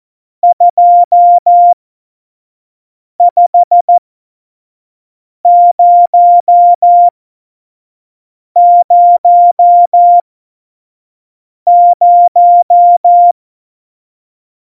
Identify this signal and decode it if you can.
Morse code, heard as a keyed tone, about 14 wpm, spaced as for 5 wpm, 700 Hz. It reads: 25000
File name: Morsekurs / Witze / Witz 22. Dezember 2020, 25 F70